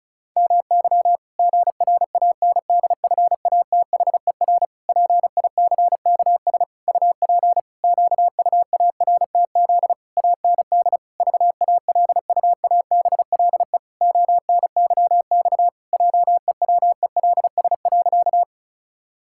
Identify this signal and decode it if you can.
Morse code, heard as a keyed tone, about 34 wpm, 700 Hz. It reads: MY GRANDFATHER PICKS UP QUARTZ AND VALUABLE ONYX JEWELS.
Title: Pangram Challenge v2 - 35wpm